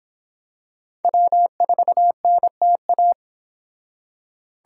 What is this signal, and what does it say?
Morse code, heard as a keyed tone, about 26 wpm, 700 Hz. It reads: W4NTA